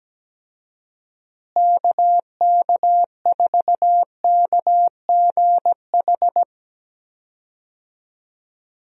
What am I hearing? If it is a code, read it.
Morse code, heard as a keyed tone, about 17 wpm, 700 Hz. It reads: KK4KGH
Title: USA Call Signs - 6 Characters - Repeated 2x - 17wpm